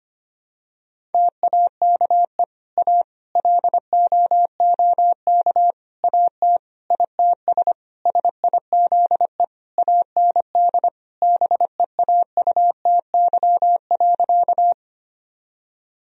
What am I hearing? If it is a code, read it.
Morse code, heard as a keyed tone, about 25 wpm, 700 Hz. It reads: TAKE A LOOK AT ITS SIZE AND BEAUTY.
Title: Sentences from Top 1000 Words - Review Part 2 - 25wpm